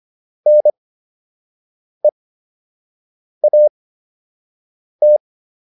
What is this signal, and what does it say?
Morse code, heard as a keyed tone, about 25 wpm, spaced as for 6 wpm, 600 Hz. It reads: NEAT